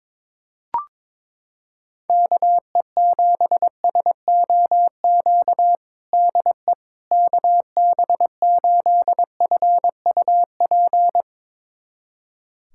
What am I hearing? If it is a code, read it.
Morse code, heard as a keyed tone, about 22 wpm, 700 Hz. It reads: KE7SOQ DE KB8FUP
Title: USA Callsign DE Callsign 22wpm